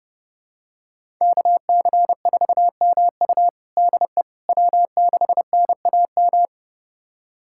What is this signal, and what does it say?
Morse code, heard as a keyed tone, about 30 wpm, 700 Hz. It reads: KC4MU DE W6NAM
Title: USA Callsign DE Callsign - 5 Characters - 30wpm